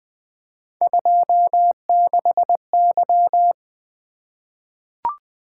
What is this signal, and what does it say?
Morse code, heard as a keyed tone, about 20 wpm, 700 Hz. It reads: 26Y
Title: Three Letter-Number Combinations - 20wpm